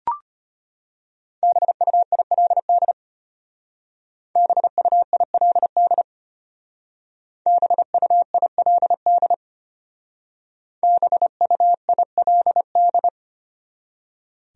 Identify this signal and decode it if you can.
Morse code, heard as a keyed tone, about 31 wpm, 700 Hz. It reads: BUILD BUILD BUILD BUILD
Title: Speed-Racing Top 300 Words 25wpm